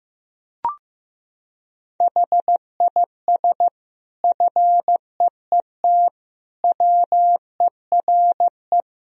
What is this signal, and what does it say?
Morse code, heard as a keyed tone, about 15 wpm, 700 Hz. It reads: HIS FEET WERE
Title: Sentences from Top 1000 Words - Review Part 1 - 15wpm